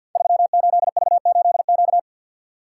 Morse code, heard as a keyed tone, about 53 wpm, 700 Hz. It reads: 38V8Q
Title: Code Groups - 5 Letter:Numbers - 50wpm